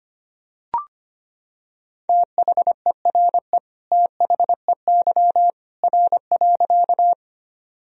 Morse code, heard as a keyed tone, about 25 wpm, 700 Hz. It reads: THERE THEY R.
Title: Sentences from Top 100 Words and Abbreviations 25wpm